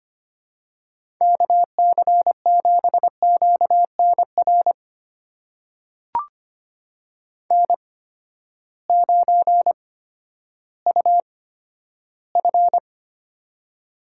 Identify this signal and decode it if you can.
Morse code, heard as a keyed tone, about 25 wpm, spaced as for 12 wpm, 700 Hz. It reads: KC7QNR N9UF
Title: USA Call Signs - 8x Character Spacing - 25wpm